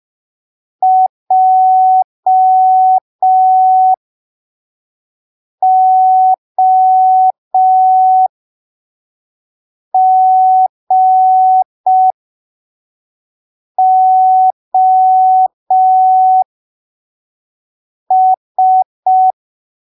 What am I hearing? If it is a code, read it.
Morse code, heard as a keyed tone, about 5 wpm, spaced as for 3 wpm, 750 Hz. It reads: JOGOS
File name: arquivos / texto / texto 08 05ppm